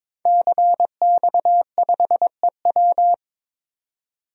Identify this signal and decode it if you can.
Morse code, heard as a keyed tone, about 22 wpm, 700 Hz. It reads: CX5EW
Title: DXCC-100 Call Signs - 5 Characters - 22wpm